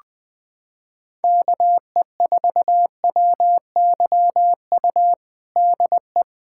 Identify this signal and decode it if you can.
Morse code, heard as a keyed tone, about 20 wpm, 700 Hz. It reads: KE4WYU DE
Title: USA Callsign DE Callsign - 6 Characters - 20wpm